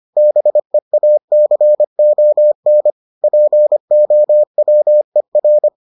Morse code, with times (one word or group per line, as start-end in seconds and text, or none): 0.2-2.9 BEACON
3.2-5.7 POWER